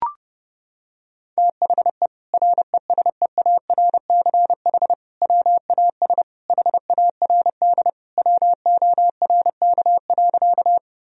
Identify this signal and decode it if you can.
Morse code, heard as a keyed tone, about 30 wpm, 700 Hz. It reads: THE RESEARCH WAS HARD WORK.
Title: Sentences from Top 1000 Words - Review Part 4 - 30wpm